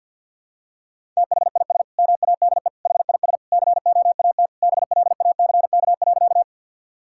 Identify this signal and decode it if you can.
Morse code, heard as a keyed tone, about 50 wpm, 700 Hz. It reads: THIS MADE HIS COAT BLACK.
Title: Sentences from Top 1600 Words 50wpm